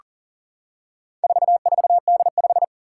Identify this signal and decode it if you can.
Morse code, heard as a keyed tone, about 42 wpm, 700 Hz. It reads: 44D5
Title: Four Challenging Letter-Number Combinations 40wpm